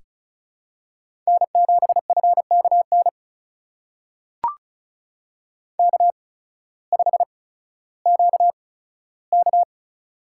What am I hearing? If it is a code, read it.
Morse code, heard as a keyed tone, about 34 wpm, spaced as for 16 wpm, 700 Hz. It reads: N7FKN K5QK
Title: USA Call Signs - 8x Character Spacing - 35wpm